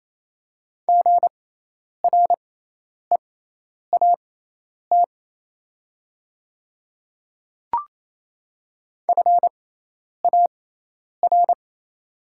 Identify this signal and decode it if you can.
Morse code, heard as a keyed tone, about 28 wpm, spaced as for 10 wpm, 700 Hz. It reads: GREAT FAR
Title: 6x Character Spacing - Top 500 Words - 28wpm